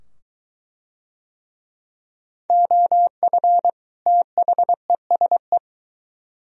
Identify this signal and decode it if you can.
Morse code, heard as a keyed tone, about 22 wpm, 700 Hz. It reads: OF THESE